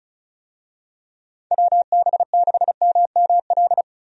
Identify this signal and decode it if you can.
Morse code, heard as a keyed tone, about 34 wpm, 700 Hz. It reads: WB6MML
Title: USA Call Signs - 6 Characters - Repeated 3x - 35wpm